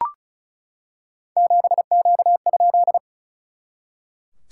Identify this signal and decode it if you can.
Morse code, heard as a keyed tone, about 36 wpm, 700 Hz. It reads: 7Q?
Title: SOTA QSO Practice 35wpm